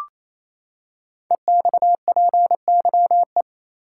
Morse code, heard as a keyed tone, about 27 wpm, 700 Hz. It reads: EXPYE
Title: Code Groups - 5 Letters - 28wpm